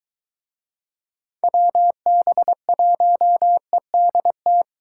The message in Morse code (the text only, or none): WB1EDT